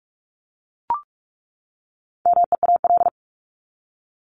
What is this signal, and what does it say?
Morse code, heard as a keyed tone, about 44 wpm, 700 Hz. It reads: MEAL
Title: Top 1751-2000 Words 45wpm